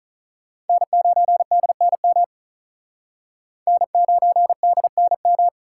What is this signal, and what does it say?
Morse code, heard as a keyed tone, about 46 wpm, 700 Hz. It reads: N9DNM N9DNM